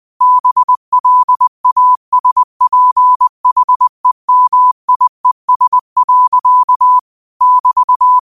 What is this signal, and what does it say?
Morse code, heard as a keyed tone, about 20 wpm, 1000 Hz. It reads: BLASPHEMIES. <BT>
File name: qotdfiles / 20wpm